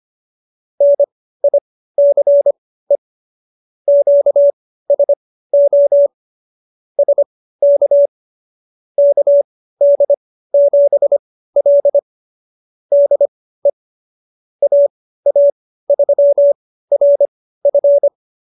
Morse code, heard as a keyed tone, about 25 wpm, spaced as for 15 wpm, 570 Hz. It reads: NICE QSO SK KD7L DE AA3RF
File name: qso222 15